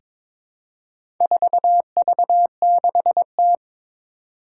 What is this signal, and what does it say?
Morse code, heard as a keyed tone, about 22 wpm, 700 Hz. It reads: 4V6T
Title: Four Challenging Letter-Number Combinations 22wpm